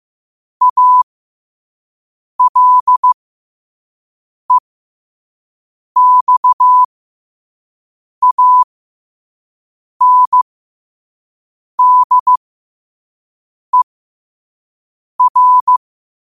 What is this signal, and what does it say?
Morse code, heard as a keyed tone, about 15 wpm, spaced as for 5 wpm, 1000 Hz. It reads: ALEXANDER
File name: qotdfiles / 5wpm